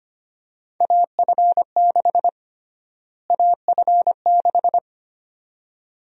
Morse code, heard as a keyed tone, about 25 wpm, 700 Hz. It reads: AF6 AF6
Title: USA Call Sign Prefixes - Repeated 2x - 25wpm